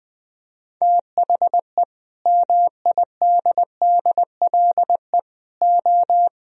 Morse code, heard as a keyed tone, about 22 wpm, 700 Hz. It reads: THE MIDDLE O